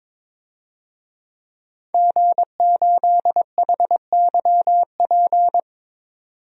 Morse code, heard as a keyed tone, about 22 wpm, 700 Hz. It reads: G8HYP
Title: DXCC-100 Call Signs - 5 Characters - 22wpm